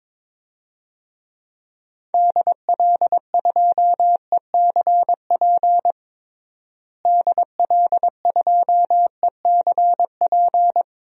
Morse code, heard as a keyed tone, about 22 wpm, 700 Hz. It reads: DL2ECP DL2ECP
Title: DXCC-100 Call Signs - 6 Characters - Repeated 3x - 22wpm